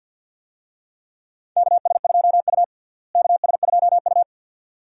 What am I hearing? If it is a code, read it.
Morse code, heard as a keyed tone, about 52 wpm, 700 Hz. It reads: KS2U KS2U
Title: USA Call Signs - 4 Characters - Repeated 2x - 50wpm